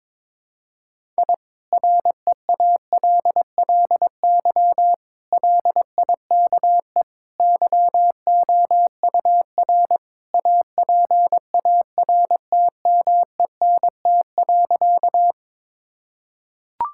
Morse code, heard as a keyed tone, about 22 wpm, 700 Hz. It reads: I REALLY LIKE YOUR APARTMENT.
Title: Sentences from Top 1000 Words - Review Part 6 - 22wpm